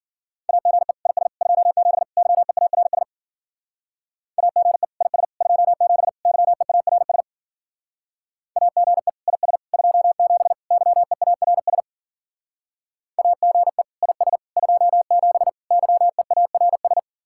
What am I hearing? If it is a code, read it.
Morse code, heard as a keyed tone, about 52 wpm, 700 Hz. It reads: AGE IS 27 YEARS AGE IS 27 YEARS AGE IS 27 YEARS AGE IS 27 YEARS